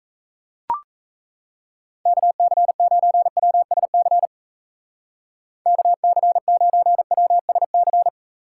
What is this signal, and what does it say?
Morse code, heard as a keyed tone, about 35 wpm, 700 Hz. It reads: KC9WSC KC9WSC